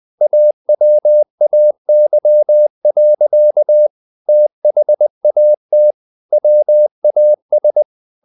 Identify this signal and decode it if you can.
Morse code, heard as a keyed tone, about 20 wpm, 600 Hz. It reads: AWAY. THAT WAS
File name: the old man and the sea part 2 20wpm